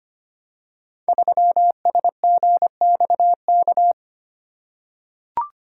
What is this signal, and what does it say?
Morse code, heard as a keyed tone, about 25 wpm, 700 Hz. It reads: 3SGXK
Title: Code Groups - 5 Letter:Numbers - 25wpm